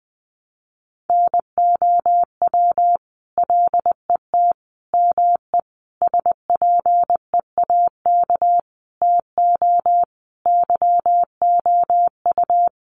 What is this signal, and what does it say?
Morse code, heard as a keyed tone, about 19 wpm, 700 Hz. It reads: NOW LET ME SPEAK TO YOU
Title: Sentences from Top 1200 Words 20wpm